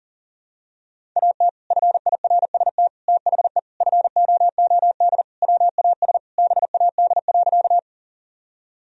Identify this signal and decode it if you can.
Morse code, heard as a keyed tone, about 40 wpm, 700 Hz. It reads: AT FIRST THE FOOD WAS BAD.
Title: Sentences from Top 1000 Words - Review Part 1 - 40wpm